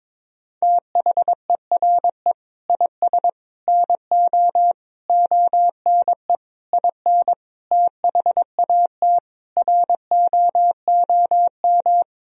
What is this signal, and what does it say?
Morse code, heard as a keyed tone, about 22 wpm, 700 Hz. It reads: THERE IS NO ONE IN THAT ROOM
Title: Sentences from Top 500 Words and Abbreviations 22wpm